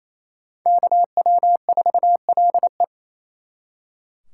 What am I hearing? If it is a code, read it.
Morse code, heard as a keyed tone, about 35 wpm, 700 Hz. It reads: KW4LE
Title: Speed Racing - USA 2x2 Call Signs 28wpm